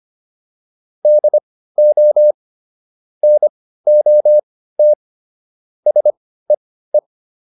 Morse code, heard as a keyed tone, about 25 wpm, spaced as for 15 wpm, 600 Hz. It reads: DO NOT SEE
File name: WD204 15